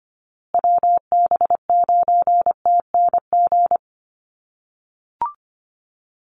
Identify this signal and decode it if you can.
Morse code, heard as a keyed tone, about 25 wpm, 700 Hz. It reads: WB9TNG
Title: USA Call Signs Repeated 3x 25wpm